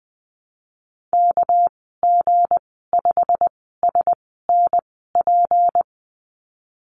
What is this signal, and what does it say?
Morse code, heard as a keyed tone, about 20 wpm, spaced as for 15 wpm, 700 Hz. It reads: KG5SNP
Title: 231 Lesson 37g - Letter x - Call Signs 20wpm